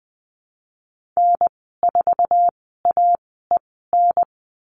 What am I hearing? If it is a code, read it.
Morse code, heard as a keyed tone, about 20 wpm, spaced as for 15 wpm, 700 Hz. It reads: N4AEN
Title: 048 Lesson 9g - Number 4 - Call Signs 20wpm